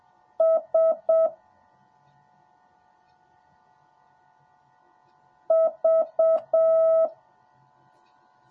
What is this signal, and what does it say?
Morse code, heard as a keyed tone, about 7 wpm, 650 Hz. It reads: S V
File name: jh8cbh / 2021cw / cw10a